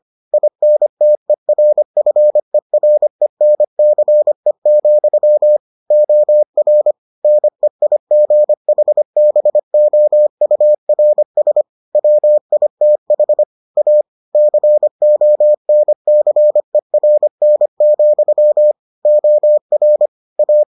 Morse code, heard as a keyed tone, about 25 wpm, 600 Hz. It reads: INTERFERENCE, OR NEIGHBOURS WITH A CONCERN, OR A